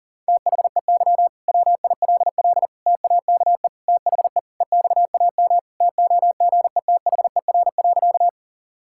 Morse code, heard as a keyed tone, about 40 wpm, 700 Hz. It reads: THEY WILL TAKE THE EXAM TOGETHER.
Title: Sentences from Top 1000 Words - Review Part 5 - 40wpm